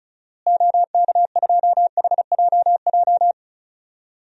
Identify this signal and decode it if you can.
Morse code, heard as a keyed tone, about 36 wpm, 700 Hz. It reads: OK2HJJ
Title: DXCC-100 Call Signs - 6 Characters - Repeated 2x - 35wpm